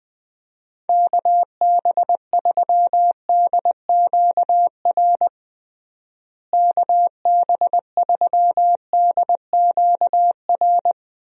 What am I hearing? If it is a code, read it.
Morse code, heard as a keyed tone, about 20 wpm, 700 Hz. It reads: KB3DQR KB3DQR
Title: USA Call Signs - 6 Characters - Repeated 3x - 20wpm